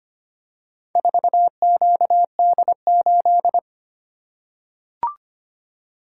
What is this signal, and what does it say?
Morse code, heard as a keyed tone, about 25 wpm, 700 Hz. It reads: 4QD8
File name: practice / 25wpm / Four Letter-Number Combinations - 25wpm